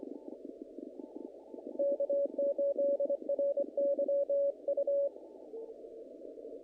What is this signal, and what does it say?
Morse code, heard as a keyed tone, about 24 wpm, 560 Hz. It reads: K8RYU
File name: contests / naqp-cw-k5tr-op-k5pi- -1800